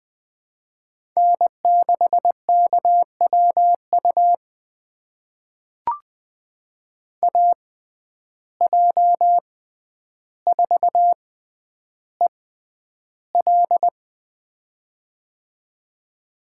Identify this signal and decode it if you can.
Morse code, heard as a keyed tone, about 20 wpm, spaced as for 10 wpm, 700 Hz. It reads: N6KWU AJ4EL
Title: USA Call Signs - 5 Characters - 6x Character Spacing - 20wpm